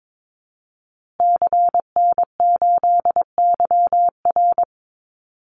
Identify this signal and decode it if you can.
Morse code, heard as a keyed tone, about 22 wpm, 700 Hz. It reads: CN8YR
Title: Call Signs Repeated 2x - 22wpm